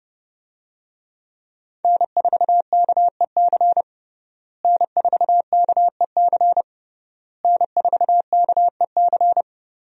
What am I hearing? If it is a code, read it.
Morse code, heard as a keyed tone, about 30 wpm, 700 Hz. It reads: N4KEC N4KEC N4KEC
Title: USA Call Signs - 5 Characters - Repeated 3x - 30wpm